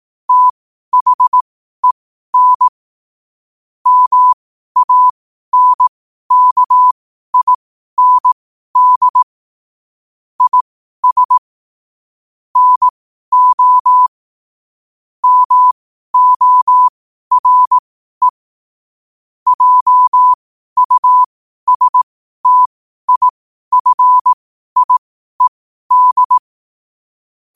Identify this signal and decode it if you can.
Morse code, heard as a keyed tone, about 18 wpm, spaced as for 12 wpm, 1000 Hz. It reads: THEN MANKIND IS NO MORE JUSTIFIED
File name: qotdfiles / 12wpm